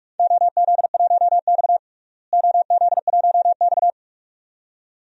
Morse code, heard as a keyed tone, about 46 wpm, 700 Hz. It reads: OZ1X OZ1X
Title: DXCC-100 Call Signs - 4 Characters - Repeated 2x - 45wpm